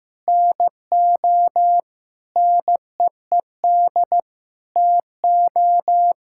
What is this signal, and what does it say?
Morse code, heard as a keyed tone, about 15 wpm, 700 Hz. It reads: NO NEED TO
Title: Sentences from Top 1600 Words 15wpm